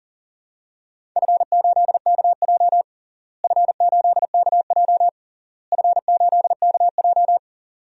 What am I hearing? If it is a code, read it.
Morse code, heard as a keyed tone, about 37 wpm, 700 Hz. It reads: F8KJ F8KJ F8KJ
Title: DXCC-100 Call Signs - 4 Characters - Repeated 3x - 40wpm